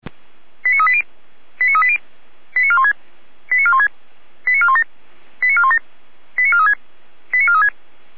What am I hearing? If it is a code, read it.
Five-tone selective call (ZVEI), heard as quick runs of five tones, tones of about 70 ms. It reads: 89380 89380 89416 89416 89417 89417 89436 89436